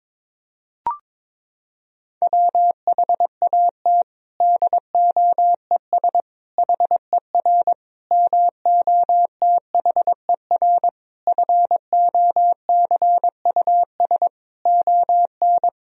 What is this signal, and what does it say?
Morse code, heard as a keyed tone, about 21 wpm, 700 Hz. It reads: WHAT DOES HER MOTHER FOCUS ON
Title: Sentences from Top 1000 Words - Review Part 4 - 22wpm